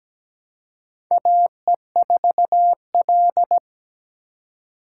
Morse code, heard as a keyed tone, about 17 wpm, 700 Hz. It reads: AE4L
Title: USA Call Signs - 4 Characters - 4x Character Spacing - 17wpm